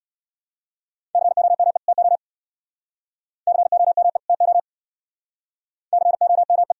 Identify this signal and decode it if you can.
Morse code, heard as a keyed tone, about 59 wpm, 700 Hz. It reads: COME TO COME TO COME
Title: Speed-Racing - Sets of 2 Words 50wpm